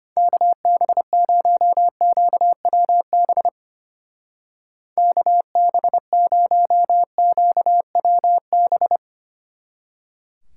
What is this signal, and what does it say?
Morse code, heard as a keyed tone, about 29 wpm, 700 Hz. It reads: KB0QWB KB0QWB
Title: Speed Racing - USA 2x3 Call Signs 25wpm